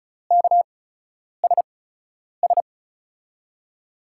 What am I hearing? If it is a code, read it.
Morse code, heard as a keyed tone, about 34 wpm, spaced as for 10 wpm, 700 Hz. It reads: KSS
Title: USA Call Signs - 6 Characters - 8x Character Spacing - 35wpm